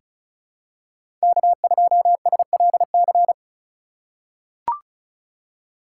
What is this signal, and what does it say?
Morse code, heard as a keyed tone, about 36 wpm, 700 Hz. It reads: K2SLC
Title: USA Call Signs - 5 Characters - 2x Character Spacing - 35wpm